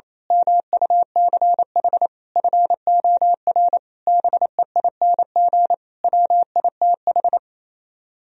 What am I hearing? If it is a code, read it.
Morse code, heard as a keyed tone, about 26 wpm, 700 Hz. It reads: MUCH FOR BEING WITH
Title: Speed-Racing - Sets of 5 Words 28wpm